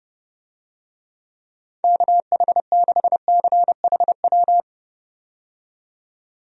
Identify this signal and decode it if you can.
Morse code, heard as a keyed tone, about 32 wpm, 700 Hz. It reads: KH6CHW
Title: USA Call Signs - 6 Characters - 30wpm